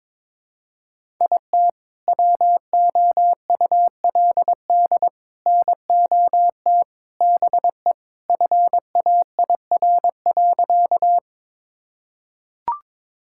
Morse code, heard as a keyed tone, about 21 wpm, 700 Hz. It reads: IT WOULD NOT BE FAIR.
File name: practice / 22wpm / Sentences with 3-Word Phrases - 22wpm